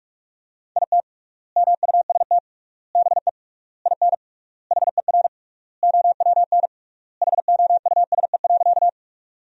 Morse code, heard as a keyed tone, about 46 wpm, 700 Hz. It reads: IT MUST BE IN HER OWN HOUSE.
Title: 3x Word-Spacing - Sentences from Top 300 Words - 45wpm